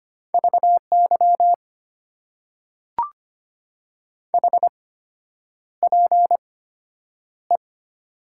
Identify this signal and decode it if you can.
Morse code, heard as a keyed tone, about 25 wpm, spaced as for 9 wpm, 700 Hz. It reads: VY HPE
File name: Top 90 QSO Elements - 8x Character Spacing - 25wpm